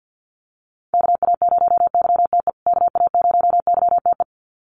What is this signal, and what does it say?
Morse code, heard as a keyed tone, about 50 wpm, 700 Hz. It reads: KA0YTE KA0YTE
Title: USA Call Signs Repeated 2x 50wpm